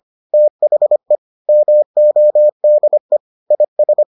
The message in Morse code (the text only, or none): THE MODE IS